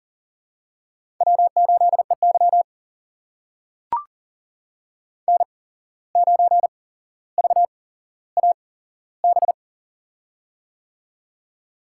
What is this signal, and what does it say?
Morse code, heard as a keyed tone, about 40 wpm, spaced as for 16 wpm, 700 Hz. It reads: W8EY N9VAB
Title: USA Call Signs - 8x Character Spacing - 40wpm